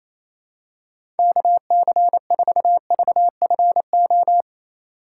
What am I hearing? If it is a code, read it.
Morse code, heard as a keyed tone, about 28 wpm, 700 Hz. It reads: KC4VFO
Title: USA Call Signs - 6 Characters - 4x Character Spacing - 28wpm